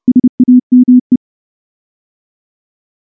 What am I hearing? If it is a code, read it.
Morse code, heard as a keyed tone, about 29 wpm, 250 Hz. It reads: SAME
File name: morse / morse-training-30WPM